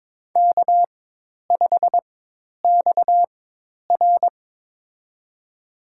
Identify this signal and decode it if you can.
Morse code, heard as a keyed tone, about 22 wpm, spaced as for 10 wpm, 700 Hz. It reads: K5XR
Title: USA Call Signs - 4 Characters - 4x Character Spacing - 22wpm